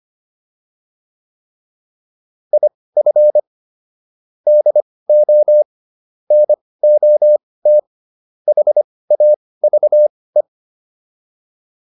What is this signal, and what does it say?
Morse code, heard as a keyed tone, about 25 wpm, spaced as for 18 wpm, 600 Hz. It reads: IF DO NOT HAVE
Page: intermediate-practice-files